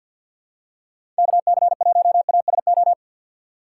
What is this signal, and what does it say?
Morse code, heard as a keyed tone, about 51 wpm, 700 Hz. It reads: KC1ASO
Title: USA Call Signs - 2x Character Spacing - 50wpm